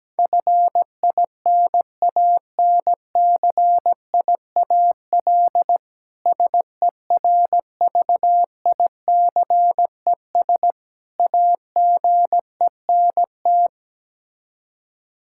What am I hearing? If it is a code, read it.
Morse code, heard as a keyed tone, about 17 wpm, 700 Hz. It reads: FINANCIAL SERVICES AGENT
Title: Practice QSO - Exchange 3 - More Info - 17wpm